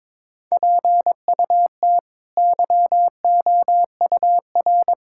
Morse code, heard as a keyed tone, about 22 wpm, 700 Hz. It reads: PUT YOUR
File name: practice / 22wpm / Sets of 8 Words - 22wpm